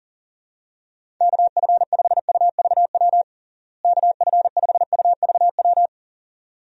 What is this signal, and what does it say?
Morse code, heard as a keyed tone, about 40 wpm, 700 Hz. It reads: KF5UVW KF5UVW
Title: USA Call Signs - 6 Characters - Repeated 2x - 40wpm